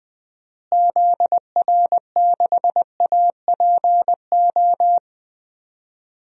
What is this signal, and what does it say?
Morse code, heard as a keyed tone, about 20 wpm, 700 Hz. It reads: ZR6APO
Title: DXCC-100 Call Signs - 6 Characters - Repeated 3x - 20wpm